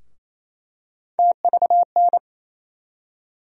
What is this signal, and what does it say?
Morse code, heard as a keyed tone, about 28 wpm, 700 Hz. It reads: TVN
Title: USA Call Sign Postfixes - 3 Letters 28wpm